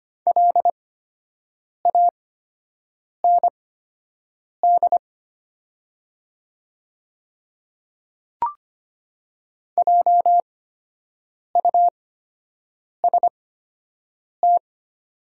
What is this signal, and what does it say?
Morse code, heard as a keyed tone, about 25 wpm, spaced as for 7 wpm, 700 Hz. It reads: LAND JUST
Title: 8x Character Spacing - Top 300 Words - 25wpm